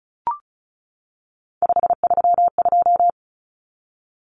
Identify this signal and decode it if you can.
Morse code, heard as a keyed tone, about 34 wpm, 700 Hz. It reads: 532
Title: Common Cut Numbers - 3-Digit 35wpm